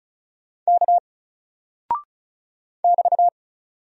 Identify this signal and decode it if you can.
Morse code, heard as a keyed tone, about 36 wpm, 700 Hz. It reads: K <BT>
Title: Single Character (Letter:Number:Punctuation:Prosign) Alt 35wpm